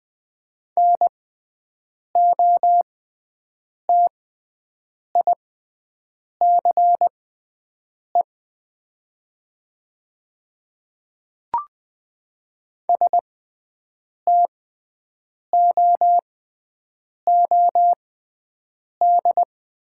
Morse code, heard as a keyed tone, about 20 wpm, spaced as for 7 wpm, 700 Hz. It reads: NOTICE STOOD
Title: 6x Character Spacing - Top 500 Words - 20wpm